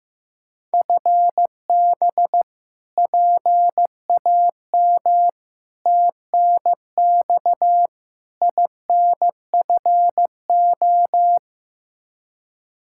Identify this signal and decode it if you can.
Morse code, heard as a keyed tone, about 15 wpm, 700 Hz. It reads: FB PAM TNX INFO
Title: K1USN SST QSO Practice 15wpm